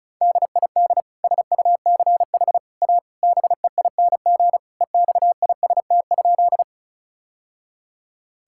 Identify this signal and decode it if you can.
Morse code, heard as a keyed tone, about 34 wpm, 700 Hz. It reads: DID SUCH A BEING EXIST?
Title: Sentences from Top 1000 Words - Review Part 5 - 35wpm